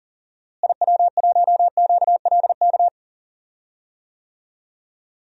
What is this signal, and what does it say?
Morse code, heard as a keyed tone, about 40 wpm, 700 Hz. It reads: IW1QLK
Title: DXCC-100 Call Signs - 6 Characters - Repeated 2x - 40wpm